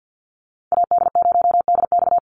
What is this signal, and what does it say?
Morse code, heard as a keyed tone, about 48 wpm, 700 Hz. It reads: AD0DX
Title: Call Signs Repeated 2x - 50wpm